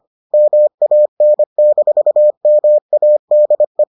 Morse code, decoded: MAN-MADE

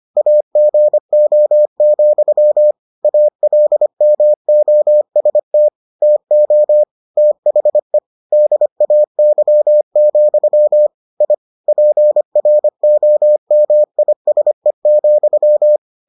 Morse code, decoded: AGO, ALMOST TO THE DAY, I PROMISE,

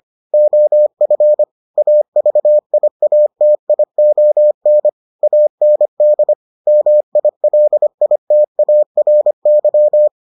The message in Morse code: OF AVIATION AND MILITARY

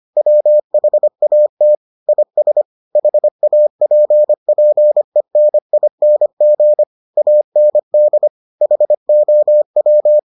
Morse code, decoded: WHAT IS HAPPENING AND HOW

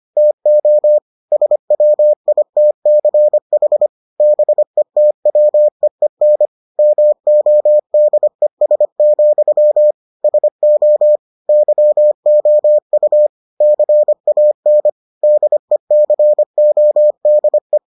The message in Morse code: TO SWITCH BETWEEN MODES, SO YOU CAN DECODE

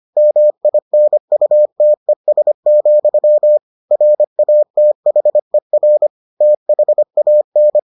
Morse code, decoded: MINUTES, RATHER THAN

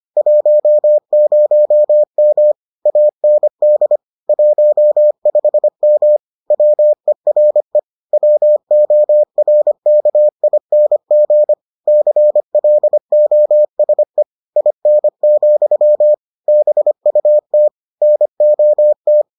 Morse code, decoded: 10M AND 15M WERE WORKING CLOSE IN, BUT NOT

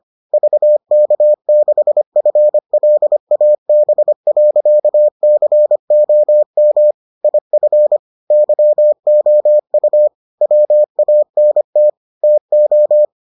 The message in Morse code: VK6FLAB.COM IF YOU WANT TO

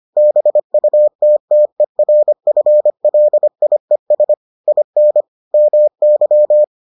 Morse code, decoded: BUTTERFLIES IN MY

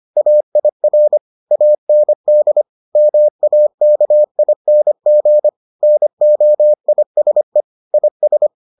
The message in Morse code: AIR AND MAKING NOISE IS

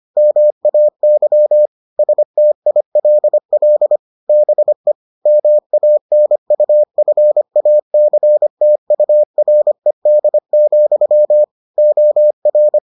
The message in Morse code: MAY STILL BE MANUFACTURED, OR

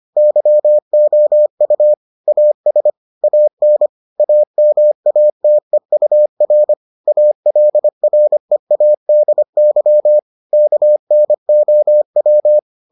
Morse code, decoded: YOU AS AN AMATEUR ALREADY KNOW

